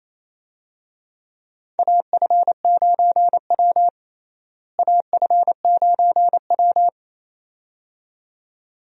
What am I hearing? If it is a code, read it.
Morse code, heard as a keyed tone, about 28 wpm, 700 Hz. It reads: AF9W AF9W
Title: USA Call Signs - 4 Characters - Repeated 2x - 28wpm